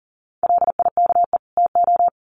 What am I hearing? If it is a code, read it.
Morse code, heard as a keyed tone, about 40 wpm, 700 Hz. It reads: LIKE TO
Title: Sets of 2 Words 40wpm